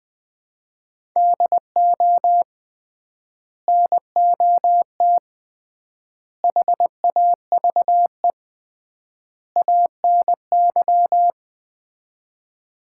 Morse code, heard as a keyed tone, about 21 wpm, 700 Hz. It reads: DO NOT HAVE ANY
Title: 3x Word-Spacing - Sets of 4 Words - 20wpm